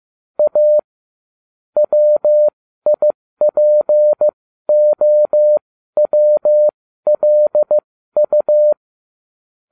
Morse code, heard as a keyed tone, about 15 wpm, spaced as for 12 wpm, 600 Hz. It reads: A WIPOWLU